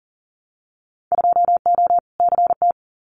Morse code, heard as a keyed tone, about 40 wpm, 700 Hz. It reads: 2O CT
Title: Winter Field Day Exchanges 40wpm